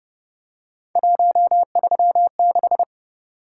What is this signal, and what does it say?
Morse code, heard as a keyed tone, about 29 wpm, 700 Hz. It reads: 136
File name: Speed-Racing - Three Letter-Number Combinations - 25wpm